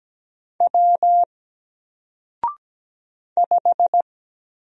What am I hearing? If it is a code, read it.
Morse code, heard as a keyed tone, about 17 wpm, 700 Hz. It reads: W 5
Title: Single Letter-Number 17wpm